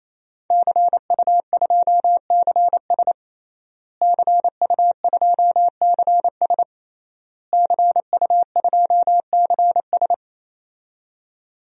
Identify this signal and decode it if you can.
Morse code, heard as a keyed tone, about 28 wpm, 700 Hz. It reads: CU2CS CU2CS CU2CS
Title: DXCC-100 Call Signs - 5 Characters - Repeated 3x - 28wpm